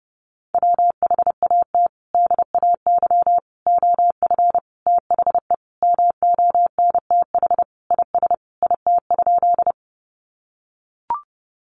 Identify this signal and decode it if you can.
Morse code, heard as a keyed tone, about 30 wpm, 700 Hz. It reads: WHAT DAY OF THE MONTH IS IT?
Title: Sentences from Top 800 Words 30wpm